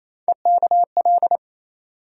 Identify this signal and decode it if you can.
Morse code, heard as a keyed tone, about 28 wpm, 700 Hz. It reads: EKL
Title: USA Call Sign Postfixes - 3 Letters 28wpm